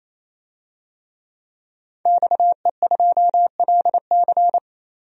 Morse code, heard as a keyed tone, about 28 wpm, 700 Hz. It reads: XE2LC
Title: DXCC-100 Call Signs - 5 Characters - 28wpm